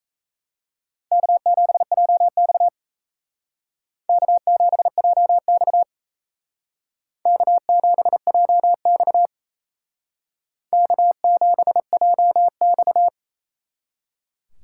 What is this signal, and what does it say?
Morse code, heard as a keyed tone, about 34 wpm, 700 Hz. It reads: K7JX K7JX K7JX K7JX
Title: Speed Racing - USA 1x2 Call Signs 28wpm